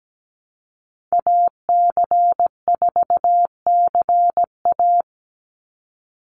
Morse code, heard as a keyed tone, about 17 wpm, 700 Hz. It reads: AC4CA
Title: Call Signs Repeated 3x - 17wpm